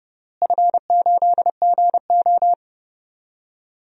Morse code, heard as a keyed tone, about 30 wpm, 700 Hz. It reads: F8GO
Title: DXCC-100 Call Signs - 4 Characters - Repeated 3x - 30wpm